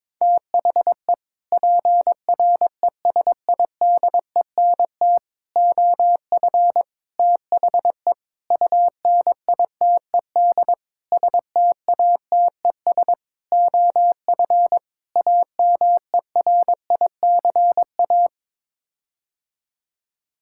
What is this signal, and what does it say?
Morse code, heard as a keyed tone, about 22 wpm, 700 Hz. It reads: THE PRESIDENT OF THE UNITED STATES OF AMERICA
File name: practice / 22wpm / Sets of 8 Words - 22wpm